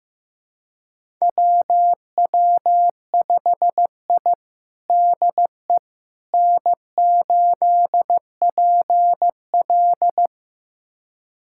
Morse code, heard as a keyed tone, about 15 wpm, 700 Hz. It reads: WW5I DE N8PL
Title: USA Callsign DE Callsign - 4 Characters - 15wpm